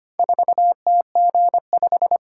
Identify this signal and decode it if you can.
Morse code, heard as a keyed tone, about 25 wpm, 700 Hz. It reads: 4TG5
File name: practice / 25wpm / Four Letter-Number Combinations - 25wpm